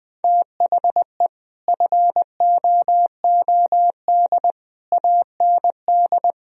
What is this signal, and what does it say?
Morse code, heard as a keyed tone, about 21 wpm, 700 Hz. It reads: THE FOOD AND